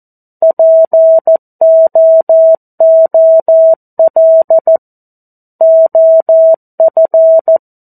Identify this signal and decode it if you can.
Morse code, heard as a keyed tone, about 14 wpm, 650 Hz. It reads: POOL OF
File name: 02-Chapter-02 14 WPM